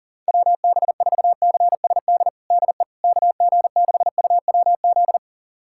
Morse code, heard as a keyed tone, about 40 wpm, 700 Hz. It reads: WB4CSD DE KG6UWZ
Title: USA Callsign DE Callsign 40wpm